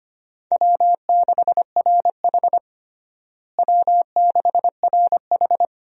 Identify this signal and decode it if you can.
Morse code, heard as a keyed tone, about 25 wpm, 700 Hz. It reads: W6RH W6RH
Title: USA Call Signs - 4 Characters - Repeated 3x - 25wpm